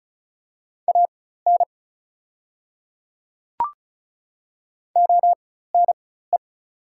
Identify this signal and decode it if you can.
Morse code, heard as a keyed tone, about 36 wpm, spaced as for 16 wpm, 690 Hz. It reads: AN ONE
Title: 4x Character Spacing - Top 100 Words - 35wpm